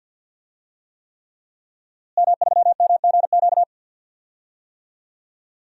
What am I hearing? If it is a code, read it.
Morse code, heard as a keyed tone, about 52 wpm, 700 Hz. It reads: M3MGQ
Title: DXCC-100 Call Signs - 5 Characters - Repeated 3x - 50wpm